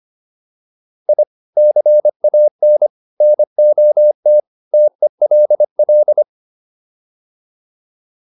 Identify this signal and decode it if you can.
Morse code, heard as a keyed tone, about 25 wpm, 600 Hz. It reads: I CAN NOT TELL